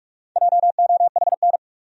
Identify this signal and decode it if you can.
Morse code, heard as a keyed tone, about 46 wpm, 700 Hz. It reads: JOHN
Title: USA Common Names - Max 4 Letters - 45wpm